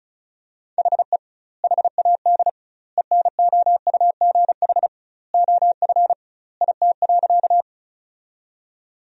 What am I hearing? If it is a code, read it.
Morse code, heard as a keyed tone, about 34 wpm, 700 Hz. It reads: HE HAD ENOUGH OF IT.
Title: 2x Word-Spacing - Sentences from Top 400 Words - 35wpm